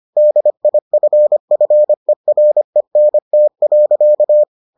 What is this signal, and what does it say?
Morse code, heard as a keyed tone, about 25 wpm, 600 Hz. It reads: DIFFERENT.